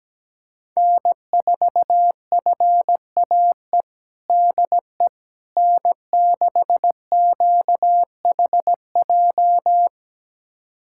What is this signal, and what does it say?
Morse code, heard as a keyed tone, about 17 wpm, 700 Hz. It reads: N4FAE DE N6QHJ
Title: USA Callsign DE Callsign - 5 Characters - 17wpm